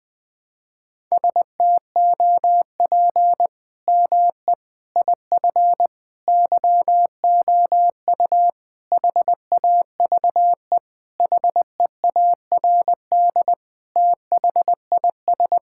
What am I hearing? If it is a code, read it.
Morse code, heard as a keyed tone, about 21 wpm, 700 Hz. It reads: STOP ME IF YOU HAVE HEARD THIS
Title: Sentences with 8-Word Phrases - 20wpm